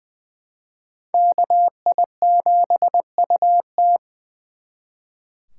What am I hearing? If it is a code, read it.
Morse code, heard as a keyed tone, about 24 wpm, 700 Hz. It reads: KI7UT
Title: Speed Racing - USA Call Signs 20wpm